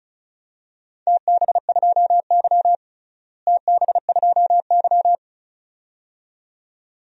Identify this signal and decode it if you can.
Morse code, heard as a keyed tone, about 36 wpm, 700 Hz. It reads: TB2Y TB2Y
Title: DXCC-100 Call Signs - 4 Characters - Repeated 2x - 35wpm